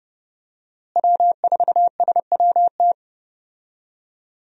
Morse code, heard as a keyed tone, about 32 wpm, 700 Hz. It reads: W4SWT
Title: USA Call Signs - 5 Characters - Repeated 2x - 30wpm